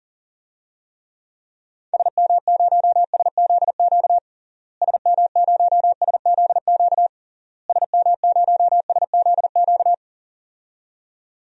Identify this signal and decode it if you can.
Morse code, heard as a keyed tone, about 40 wpm, 700 Hz. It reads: SM0SZQ SM0SZQ SM0SZQ
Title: DXCC-100 Call Signs - 6 Characters - Repeated 3x - 40wpm